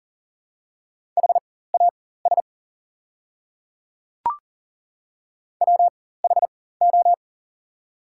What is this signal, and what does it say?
Morse code, heard as a keyed tone, about 40 wpm, spaced as for 19 wpm, 700 Hz. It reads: HAS WHO
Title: 4x Character Spacing - Top 200 Words - 40wpm